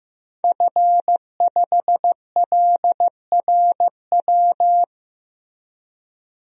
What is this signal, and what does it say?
Morse code, heard as a keyed tone, about 15 wpm, 700 Hz. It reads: F5LRW
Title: DXCC-100 Call Signs - 5 Characters - Repeated 2x - 15wpm